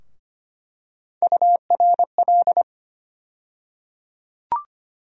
Morse code, heard as a keyed tone, about 25 wpm, 700 Hz. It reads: URL
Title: USA Call Sign Postfixes - 3 Letters 25wpm